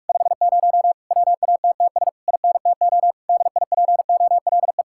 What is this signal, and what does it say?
Morse code, heard as a keyed tone, about 44 wpm, 700 Hz. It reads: 50 WATTS INTO DIPOLE